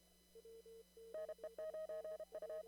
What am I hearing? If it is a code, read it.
Morse code, heard as a keyed tone, about 32 wpm, 630 Hz. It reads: NE9U
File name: CW / kt5j wpx cw-op-k5pi